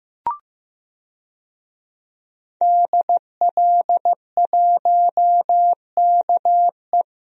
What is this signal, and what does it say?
Morse code, heard as a keyed tone, about 15 wpm, 700 Hz. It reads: DL1KE
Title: DXCC-100 Call Signs - 5 Characters - 15wpm